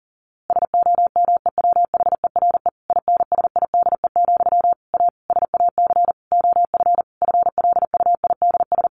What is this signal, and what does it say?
Morse code, heard as a keyed tone, about 40 wpm, 700 Hz. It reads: SOMEWHERE INSIDE, A SAC OF FLUIDS
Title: A Sound Of Thunder 40wpm